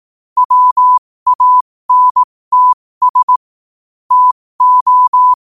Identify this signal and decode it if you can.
Morse code, heard as a keyed tone, about 18 wpm, spaced as for 15 wpm, 1000 Hz. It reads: WANTS TO
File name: qotdfiles / 15wpm